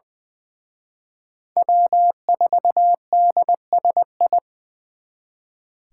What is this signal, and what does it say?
Morse code, heard as a keyed tone, about 23 wpm, 700 Hz. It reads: W4DSI
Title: Speed Racing - USA Call Signs 20wpm